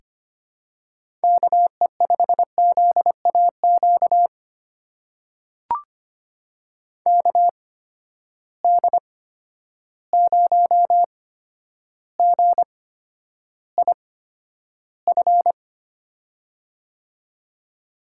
Morse code, heard as a keyed tone, about 25 wpm, spaced as for 11 wpm, 700 Hz. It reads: KE5ZAQ KD0GIF
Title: USA Call Signs - 6 Characters - 8x Character Spacing - 25wpm